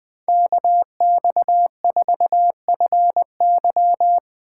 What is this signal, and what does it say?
Morse code, heard as a keyed tone, about 21 wpm, 700 Hz. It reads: KX4FY
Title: USA Call Signs - 5 Characters - 4x Character Spacing - 20wpm